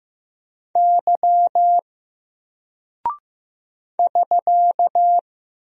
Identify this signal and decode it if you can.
Morse code, heard as a keyed tone, about 15 wpm, 700 Hz. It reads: Y <SK>
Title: Single Character (Letter:Number:Punctuation:Prosign) Alt 15wpm